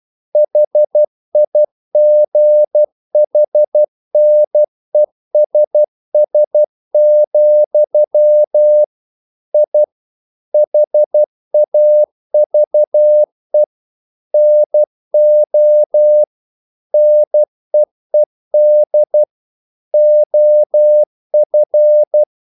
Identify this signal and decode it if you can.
Morse code, heard as a keyed tone, about 12 wpm, 600 Hz. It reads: HIGHNESS, I HAVE NO NEED OF
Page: English quotations in CW (telegraphy - morse)